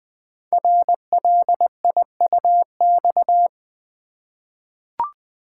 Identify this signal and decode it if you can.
Morse code, heard as a keyed tone, about 20 wpm, 700 Hz. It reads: RLIUX
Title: Code Groups - 5 Letter:Numbers - 20wpm